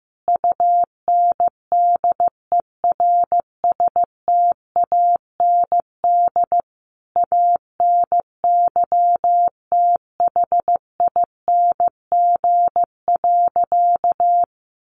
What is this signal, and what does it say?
Morse code, heard as a keyed tone, about 15 wpm, 700 Hz. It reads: UNDERSTAND ANYTHING.
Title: Sentences from Top 1000 Words 15wpm